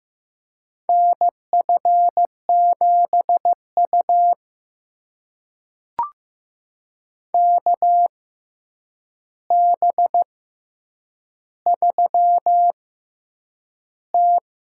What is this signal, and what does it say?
Morse code, heard as a keyed tone, about 15 wpm, spaced as for 8 wpm, 700 Hz. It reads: NF7U KB3T
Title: USA Call Signs - 6x Character Spacing - 15wpm